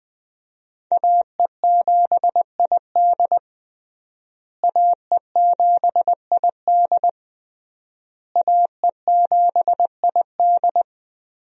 Morse code, heard as a keyed tone, about 21 wpm, 700 Hz. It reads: AE7ID AE7ID AE7ID
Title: USA Call Signs - 5 Characters - Repeated 3x - 20wpm